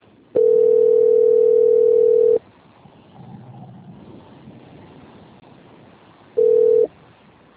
Ringback tone (North America), 440 + 480 Hz.
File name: out- - - - - .32846